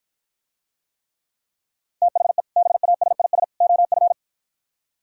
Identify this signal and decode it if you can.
Morse code, heard as a keyed tone, about 52 wpm, 700 Hz. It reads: THE BASIS OF